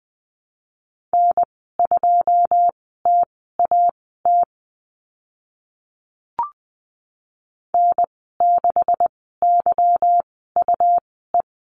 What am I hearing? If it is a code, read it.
Morse code, heard as a keyed tone, about 20 wpm, spaced as for 15 wpm, 700 Hz. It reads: N2TAT N6YUE
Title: 147 Lesson 24g - Letter y - Call Signs 20wpm